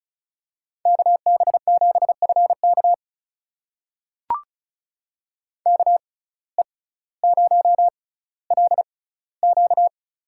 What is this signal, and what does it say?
Morse code, heard as a keyed tone, about 36 wpm, spaced as for 18 wpm, 700 Hz. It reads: KB7FK KE0LQ
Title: USA Call Signs - 5 Characters - 6x Character Spacing - 35wpm